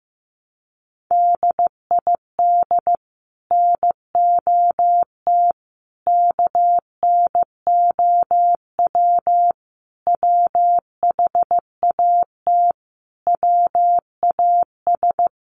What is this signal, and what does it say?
Morse code, heard as a keyed tone, about 15 wpm, 700 Hz. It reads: DID NOT KNOW WHAT WAS
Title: Sets of 5 Words Encore2 15wpm